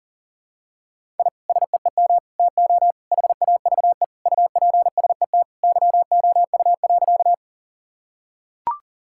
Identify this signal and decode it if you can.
Morse code, heard as a keyed tone, about 40 wpm, 700 Hz. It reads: I SEEM TO HAVE UPSET YOU.
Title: Sentences from Top 1000 Words - Review Part 5 - 40wpm